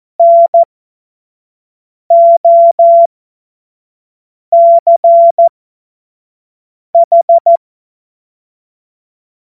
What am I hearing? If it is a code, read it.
Morse code, heard as a keyed tone, about 14 wpm, spaced as for 5 wpm, 680 Hz. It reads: NOCH
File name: Morsekurs / Witze / Witz 04. Mai 2021, 25 F70